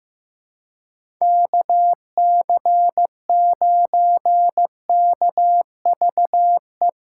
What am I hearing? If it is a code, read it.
Morse code, heard as a keyed tone, about 15 wpm, 700 Hz. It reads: KC9KVE
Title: USA Callsign DE Callsign - 6 Characters - 15wpm